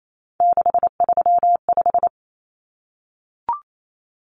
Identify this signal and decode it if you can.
Morse code, heard as a keyed tone, about 28 wpm, 700 Hz. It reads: 635
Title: Common Cut Numbers - 3-Digit 28wpm